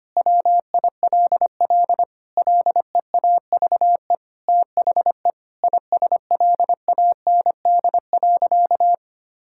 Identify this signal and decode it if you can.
Morse code, heard as a keyed tone, about 25 wpm, 700 Hz. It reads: WILL LEAVE THE ISLAND.
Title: Sentences from Top 1000 Words - Review Part 6 - 25wpm